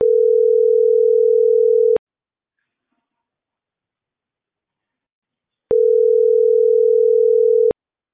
Ringback tone (North America), 440 + 480 Hz.